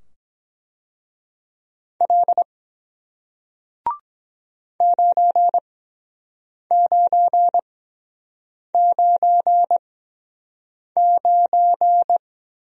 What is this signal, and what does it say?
Morse code, heard as a keyed tone, about 23 wpm, spaced as for 7 wpm, 700 Hz. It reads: L 9999